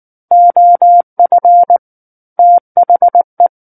Morse code, heard as a keyed tone, about 19 wpm, 700 Hz. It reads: OF THE
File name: Book One / 10-Chapter 10 - 19 WPM